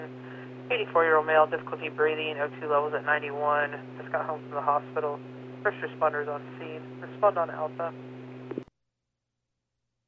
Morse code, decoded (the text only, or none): none